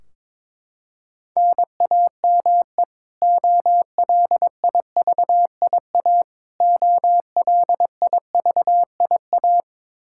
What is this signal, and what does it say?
Morse code, heard as a keyed tone, about 22 wpm, 700 Hz. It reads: NAME OLIVIA OLIVIA
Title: Practice QSO - Exchange 1 - Beginning - 22wpm